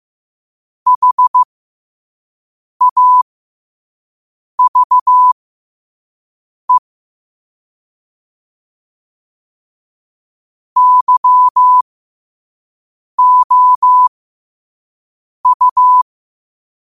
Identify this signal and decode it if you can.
Morse code, heard as a keyed tone, about 15 wpm, spaced as for 5 wpm, 1000 Hz. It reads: HAVE YOU
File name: qotdfiles / 5wpm